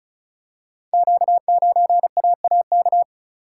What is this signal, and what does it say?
Morse code, heard as a keyed tone, about 36 wpm, 690 Hz. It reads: Q9AAK
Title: Code Groups - 5 Letter:Numbers - 35wpm